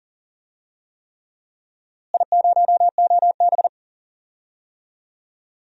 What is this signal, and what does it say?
Morse code, heard as a keyed tone, about 37 wpm, 700 Hz. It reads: I0OB
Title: DXCC-100 Call Signs - 4 Characters - Repeated 3x - 40wpm